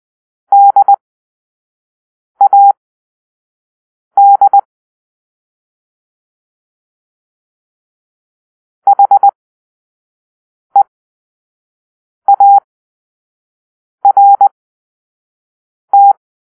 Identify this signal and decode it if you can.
Morse code, heard as a keyed tone, about 20 wpm, spaced as for 5 wpm, 800 Hz. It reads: DAD HEART
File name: arrl / Grp-2 Drill-2